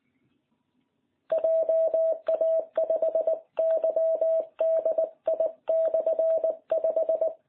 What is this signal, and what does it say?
Morse code, heard as a keyed tone, about 19 wpm, 660 Hz. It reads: JA5YDI/5